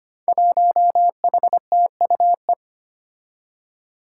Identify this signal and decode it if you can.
Morse code, heard as a keyed tone, about 25 wpm, 700 Hz. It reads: 1HTUE
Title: Five Challenging Letter-Number Combinations 25wpm